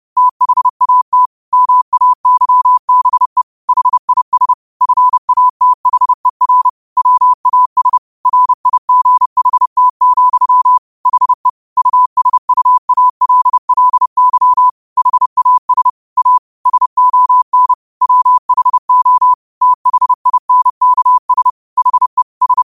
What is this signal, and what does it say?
Morse code, heard as a keyed tone, about 30 wpm, 1000 Hz. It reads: THAT MAYBE HIS FATHER WAS RIGHT, HE USUALLY HAS A SON WHO THINKS HE S
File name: qotdfiles / 30wpm